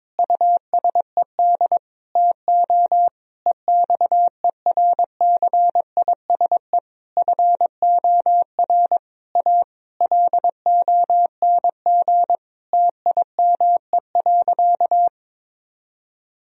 Morse code, heard as a keyed tone, about 22 wpm, 700 Hz. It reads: USED TO EXERCISE FOR A LONG TIME.
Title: Sentences from Top 1000 Words - Review Part 4 - 22wpm